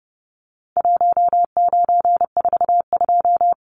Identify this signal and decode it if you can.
Morse code, heard as a keyed tone, about 30 wpm, 700 Hz. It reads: 1942
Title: Numbers - Contest Style - 4-Digit 30wpm